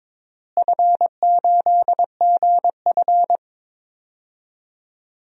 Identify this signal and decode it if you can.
Morse code, heard as a keyed tone, about 22 wpm, 700 Hz. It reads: F8GF
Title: DXCC-100 Call Signs - 4 Characters - Repeated 3x - 22wpm